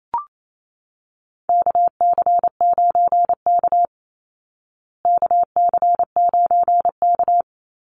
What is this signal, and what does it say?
Morse code, heard as a keyed tone, about 28 wpm, 700 Hz. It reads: KC9K KC9K
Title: Call Signs Repeated 2x - 28wpm